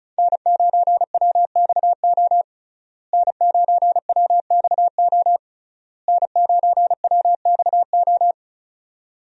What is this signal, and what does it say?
Morse code, heard as a keyed tone, about 36 wpm, 700 Hz. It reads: N9WXO N9WXO N9WXO
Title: USA Call Signs - 5 Characters - Repeated 3x - 35wpm